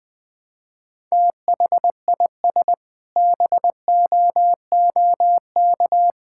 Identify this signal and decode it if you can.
Morse code, heard as a keyed tone, about 20 wpm, 700 Hz. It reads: THIS BOOK
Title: Sentences with 7-Word Phrases - Spoken Beginning - 20wpm